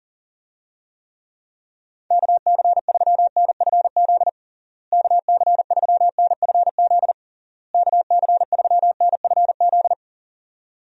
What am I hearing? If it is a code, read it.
Morse code, heard as a keyed tone, about 40 wpm, 700 Hz. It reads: KC3NFZ KC3NFZ KC3NFZ
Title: USA Call Signs - 6 Characters - Repeated 3x - 40wpm